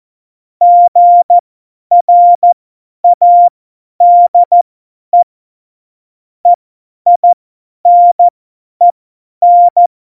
Morse code, heard as a keyed tone, about 14 wpm, spaced as for 10 wpm, 700 Hz. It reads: GRADE EINEN
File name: Morsekurs / Witze / Witz 26. Juni 2018, 50 F70